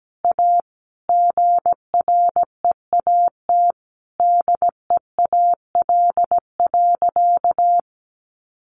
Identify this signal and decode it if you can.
Morse code, heard as a keyed tone, about 17 wpm, 700 Hz. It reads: A GREAT DEAL.
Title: Sentences from Top 1100 Words 17wpm